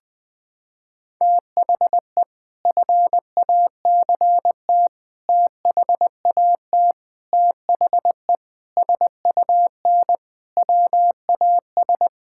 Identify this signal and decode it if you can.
Morse code, heard as a keyed tone, about 21 wpm, 700 Hz. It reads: THE FACT THAT THE SUN WAS
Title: Sentences with 4-Word Phrases - 20wpm